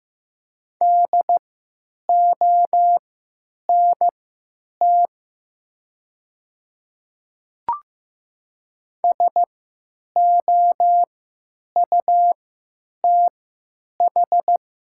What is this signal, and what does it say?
Morse code, heard as a keyed tone, about 15 wpm, spaced as for 9 wpm, 700 Hz. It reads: DONT SOUTH
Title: 3x Character Spacing - Top 500 Words - 15wpm